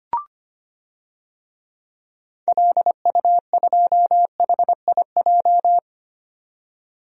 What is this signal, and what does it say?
Morse code, heard as a keyed tone, about 25 wpm, 700 Hz. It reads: LU2HIJ
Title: DXCC-100 Call Signs - 6 Characters - 25wpm